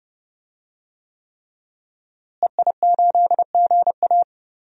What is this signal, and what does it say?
Morse code, heard as a keyed tone, about 30 wpm, 700 Hz. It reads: EI8GA
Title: DXCC-100 Call Signs - 5 Characters - Repeated 3x - 30wpm